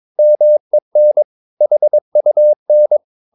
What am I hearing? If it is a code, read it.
Morse code, heard as a keyed tone, about 22 wpm, 600 Hz. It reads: MEN HUN
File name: Ferie QRS 22wpm